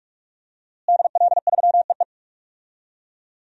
Morse code, heard as a keyed tone, about 46 wpm, 700 Hz. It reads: DL3EE
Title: DXCC-100 Call Signs - 5 Characters - Repeated 3x - 45wpm